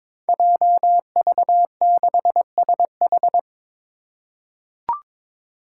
Morse code, heard as a keyed tone, about 23 wpm, 700 Hz. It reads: JV6SH
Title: Five Challenging Letter-Number Combinations 22wpm